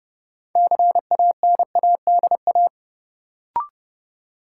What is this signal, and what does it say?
Morse code, heard as a keyed tone, about 30 wpm, 700 Hz. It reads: CANADA
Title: Sets of 5 Words - New Word Drill - 30wpm